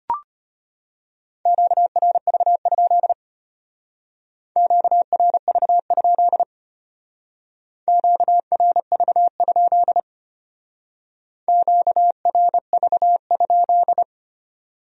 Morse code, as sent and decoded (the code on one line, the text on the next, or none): --.- .-. ...- ..--.. / --.- .-. ...- ..--.. / --.- .-. ...- ..--.. / --.- .-. ...- ..--..
QRV? QRV? QRV? QRV?